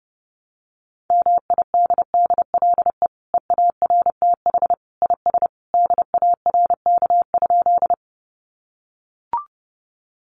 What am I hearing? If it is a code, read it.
Morse code, heard as a keyed tone, about 30 wpm, 700 Hz. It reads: MIDDLE EARTH IS DARK?
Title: Sentences from Top 900 Words 30wpm